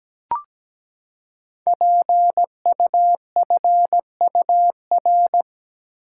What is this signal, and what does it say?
Morse code, heard as a keyed tone, about 17 wpm, 700 Hz. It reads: PUFUR
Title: Code Groups - 5 Letters - 17wpm